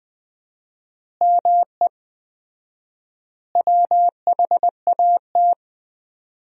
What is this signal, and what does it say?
Morse code, heard as a keyed tone, about 20 wpm, 700 Hz. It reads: ME WHAT
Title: 4x Word-Spacing - Sentences from Top 200 Words - 20wpm